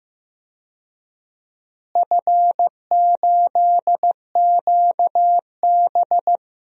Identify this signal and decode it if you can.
Morse code, heard as a keyed tone, about 15 wpm, 700 Hz. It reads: F8QB
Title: DXCC-100 Call Signs - 4 Characters - Repeated 3x - 15wpm